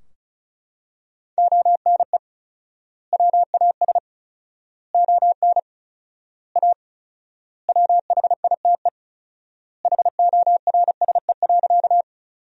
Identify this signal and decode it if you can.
Morse code, heard as a keyed tone, about 34 wpm, 700 Hz. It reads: ONE WAS ON A WHITE HORSE.
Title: 4x Word-Spacing - Sentences from Top 300 Words - 35wpm